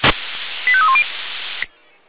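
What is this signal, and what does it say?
Five-tone selective call (ZVEI), heard as quick runs of five tones, tones of about 70 ms. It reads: 85310
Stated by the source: ZVEI five-tone selective call received on an SDR.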